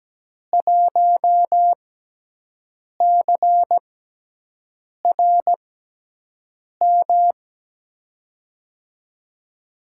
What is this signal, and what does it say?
Morse code, heard as a keyed tone, about 17 wpm, spaced as for 6 wpm, 700 Hz. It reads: 1CRM
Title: USA Call Signs - 6x Character Spacing - 17wpm